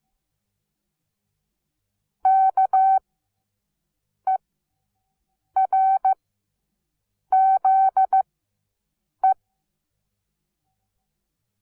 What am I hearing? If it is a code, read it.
Morse code, heard as a keyed tone, about 15 wpm, spaced as for 6 wpm, 760 Hz. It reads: KERZE
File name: Onlinelehrgang / morsen / DJ4UF Morsen17